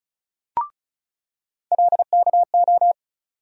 Morse code, heard as a keyed tone, about 36 wpm, 700 Hz. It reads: LKO
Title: USA Call Sign Postfixes - 3 Letters 35wpm